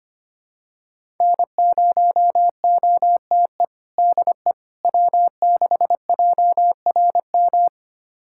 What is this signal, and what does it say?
Morse code, heard as a keyed tone, about 25 wpm, 700 Hz. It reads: N0OTE DE W6JRM
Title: USA Callsign DE Callsign - 5 Characters - 25wpm